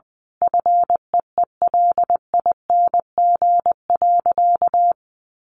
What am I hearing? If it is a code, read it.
Morse code, heard as a keyed tone, about 20 wpm, 700 Hz. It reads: FEELING.
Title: Sentences from Top 1400 Words 20wpm